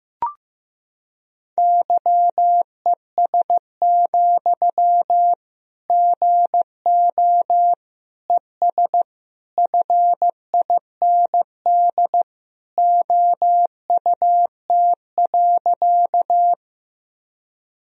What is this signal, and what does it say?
Morse code, heard as a keyed tone, about 16 wpm, 700 Hz. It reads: YES, GO ES FIND OUT.
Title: Sentences from Top 500 Words and Abbreviations 15wpm